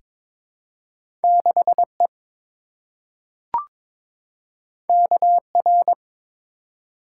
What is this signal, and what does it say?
Morse code, heard as a keyed tone, about 22 wpm, 700 Hz. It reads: 6E KR
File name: practice / 22wpm / Two Letter-Number Combinations - 22wpm